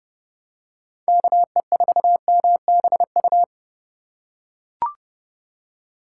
Morse code, heard as a keyed tone, about 30 wpm, 700 Hz. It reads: KE4MBU